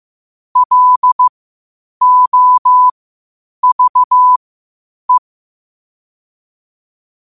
Morse code, heard as a keyed tone, about 15 wpm, spaced as for 8 wpm, 1000 Hz. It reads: LOVE
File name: qotdfiles / 8wpm